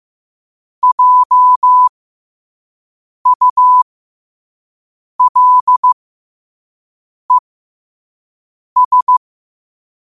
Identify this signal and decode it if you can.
Morse code, heard as a keyed tone, about 15 wpm, spaced as for 5 wpm, 1000 Hz. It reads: JULES
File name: qotdfiles / 5wpm